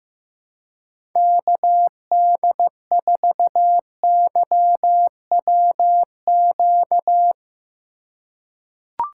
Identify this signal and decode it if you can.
Morse code, heard as a keyed tone, about 16 wpm, 700 Hz. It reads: KD4YWQ
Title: USA Call Signs - 6 Characters - 4x Character Spacing - 15wpm